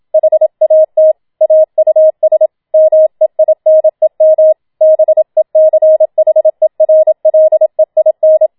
Morse code, heard as a keyed tone, about 27 wpm, 630 Hz. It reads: HAT AUS MEINEM BECHERLEIN